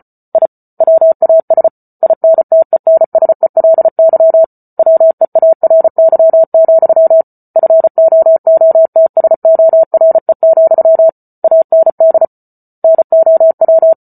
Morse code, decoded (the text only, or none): I WAS INTENSELY WEARY, FOOTSORE, AND NOW